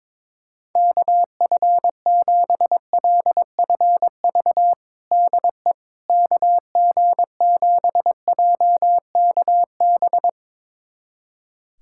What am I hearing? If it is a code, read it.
Morse code, heard as a keyed tone, about 22 wpm, 700 Hz. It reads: KF7LFV DE KG7JKB
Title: USA Callsign DE Callsign 22wpm